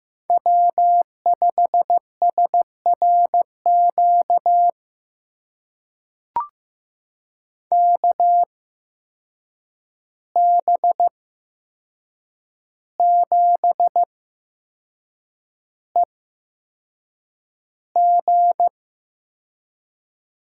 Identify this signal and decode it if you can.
Morse code, heard as a keyed tone, about 15 wpm, spaced as for 6 wpm, 700 Hz. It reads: W5SRQ KB7EG
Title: USA Call Signs - 8x Character Spacing - 15wpm